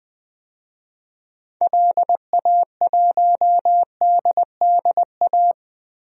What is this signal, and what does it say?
Morse code, heard as a keyed tone, about 20 wpm, 700 Hz. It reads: LA1DDA
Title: DXCC-100 Call Signs - 6 Characters - 20wpm